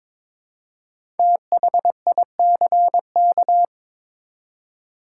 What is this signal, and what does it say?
Morse code, heard as a keyed tone, about 22 wpm, 700 Hz. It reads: THICK
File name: 5 Letter Words - First 3 Letters Given - 22wpm